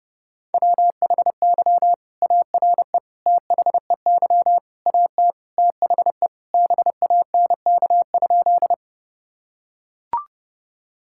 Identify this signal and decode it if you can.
Morse code, heard as a keyed tone, about 32 wpm, 700 Hz. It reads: WHY ARE THEY AT THE BANK?
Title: Sentences from Top 1000 Words - Review Part 6 - 30wpm